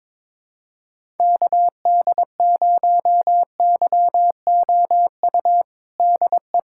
Morse code, heard as a keyed tone, about 22 wpm, 700 Hz. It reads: KD0YOU DE
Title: USA Callsign DE Callsign - 6 Characters - 22wpm